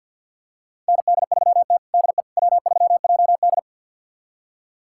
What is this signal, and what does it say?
Morse code, heard as a keyed tone, about 46 wpm, 700 Hz. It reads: ND3T DE W3JD
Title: USA Callsign DE Callsign - 4 Characters - 50wpm